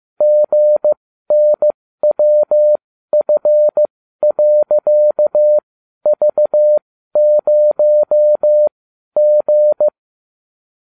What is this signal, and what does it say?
Morse code, heard as a keyed tone, about 15 wpm, spaced as for 12 wpm, 600 Hz. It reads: GNWF.V0G